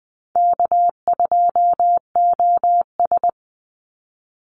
Morse code, heard as a keyed tone, about 20 wpm, 700 Hz. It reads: K2OS
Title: Call Signs Repeated 2x - 20wpm